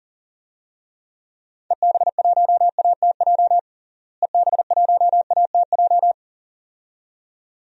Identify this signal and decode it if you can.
Morse code, heard as a keyed tone, about 40 wpm, 700 Hz. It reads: EB1ATJ EB1ATJ
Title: DXCC-100 Call Signs - 6 Characters - Repeated 2x - 40wpm